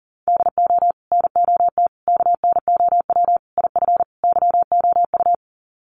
Morse code, heard as a keyed tone, about 40 wpm, 700 Hz. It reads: DO NOT KNOW IF YOU
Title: Sets of 5 Words 40wpm